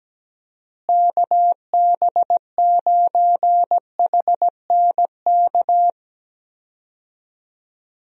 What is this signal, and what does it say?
Morse code, heard as a keyed tone, about 17 wpm, 700 Hz. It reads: KB9HNK
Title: USA Call Signs - 6 Characters - Repeated 3x - 17wpm